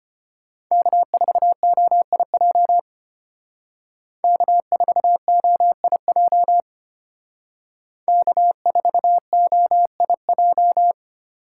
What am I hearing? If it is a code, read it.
Morse code, heard as a keyed tone, about 32 wpm, 700 Hz. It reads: K4OIJ K4OIJ K4OIJ